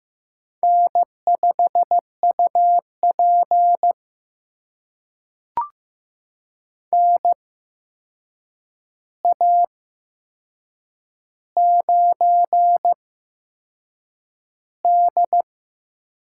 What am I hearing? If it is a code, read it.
Morse code, heard as a keyed tone, about 15 wpm, spaced as for 6 wpm, 700 Hz. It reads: N5UP NA9D
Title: USA Call Signs - 4 Characters - 8x Character Spacing - 15wpm